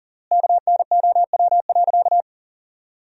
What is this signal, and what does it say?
Morse code, heard as a keyed tone, about 40 wpm, 700 Hz. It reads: KNOW.
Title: 3x Word-Spacing - Sentences from Top 400 Words - 40wpm